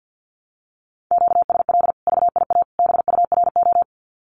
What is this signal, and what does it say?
Morse code, heard as a keyed tone, about 50 wpm, 700 Hz. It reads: QSL VIA BURO
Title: Common QSO Phrases 50wpm